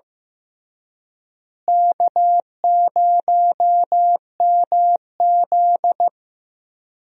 Morse code, heard as a keyed tone, about 15 wpm, 700 Hz. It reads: K0MZ
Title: USA Call Signs - 4 Characters - Repeated 3x - 15wpm